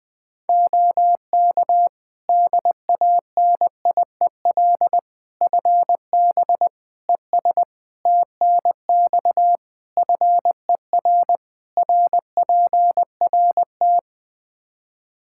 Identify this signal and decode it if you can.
Morse code, heard as a keyed tone, about 20 wpm, 700 Hz. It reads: OK DANIEL FB ES TNX FER RPRT
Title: Practice QSO - Exchange 2 - Info - 20wpm